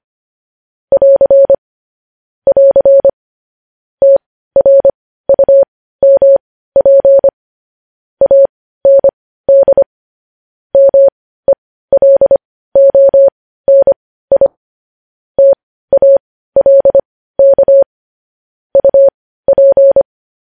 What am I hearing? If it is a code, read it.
Morse code, heard as a keyed tone, about 25 wpm, spaced as for 15 wpm, 550 Hz. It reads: <AR> <AR> TRUMP AND MELONI TALK UP
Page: News Headlines in Morse Code at 15 WPM